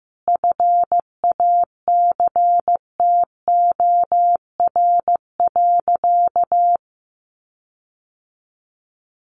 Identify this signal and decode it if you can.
Morse code, heard as a keyed tone, about 15 wpm, 700 Hz. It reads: FACTOR.
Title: Sentences from Top 1300 Words 15wpm